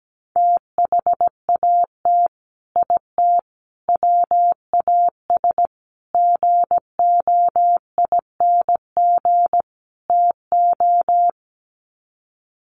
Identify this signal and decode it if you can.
Morse code, heard as a keyed tone, about 17 wpm, 700 Hz. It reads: THAT IT WAS GOING TO
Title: Sets of 5 Words Encore2 17wpm